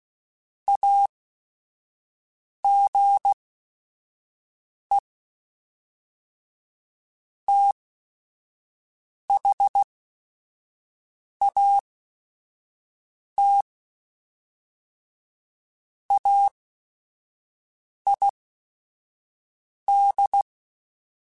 Morse code, heard as a keyed tone, about 16 wpm, spaced as for 5 wpm, 800 Hz. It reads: AGE THAT AID
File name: Technical Subjects / Technical Subjects-mp3s / XL @5